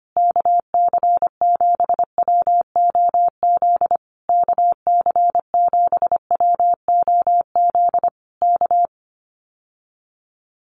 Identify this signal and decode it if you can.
Morse code, heard as a keyed tone, about 25 wpm, 700 Hz. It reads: KC7WOZ KC7WOZ K
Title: Common QSO Phrases Encore 25wpm